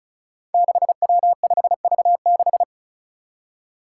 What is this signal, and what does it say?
Morse code, heard as a keyed tone, about 35 wpm, 700 Hz. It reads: 6W5V6
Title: Five Challenging Letter-Number Combinations 35wpm